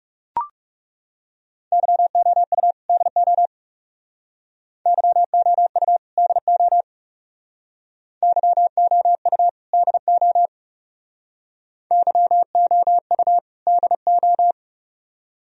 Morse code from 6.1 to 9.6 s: DO YOU

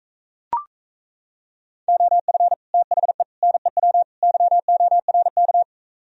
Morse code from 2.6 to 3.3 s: THE

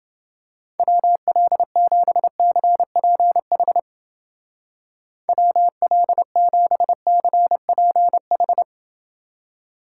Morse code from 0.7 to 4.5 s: WL7CPH

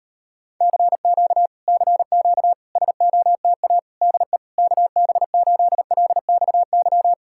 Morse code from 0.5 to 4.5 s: CQ CQ SOTA DE

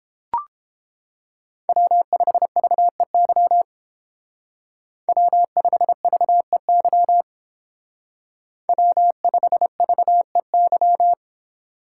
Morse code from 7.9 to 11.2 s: W5VEY

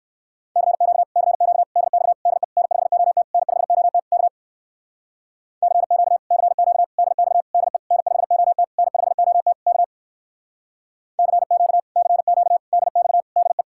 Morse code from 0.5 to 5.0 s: CQ CQ DX DE N5OT N5OT K

